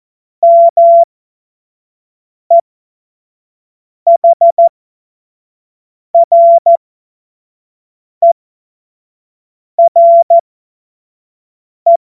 Morse code, heard as a keyed tone, about 14 wpm, spaced as for 5 wpm, 680 Hz. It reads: MEHRERE
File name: Morsekurs / Witze / Witz 03. August 2021, 25 F70